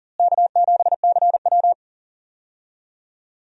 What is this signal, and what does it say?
Morse code, heard as a keyed tone, about 39 wpm, 700 Hz. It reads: K7CW
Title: USA Call Signs - 4 Characters - Repeated 2x - 40wpm